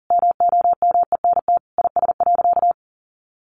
Morse code, heard as a keyed tone, about 40 wpm, 700 Hz. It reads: MOMENT IS.
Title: Sentences from Top 1000 Words 40wpm